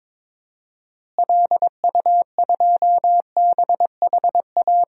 Morse code, heard as a keyed tone, about 21 wpm, 700 Hz. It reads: LU2BHA
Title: DXCC-100 Call Signs - 6 Characters - Repeated 2x - 22wpm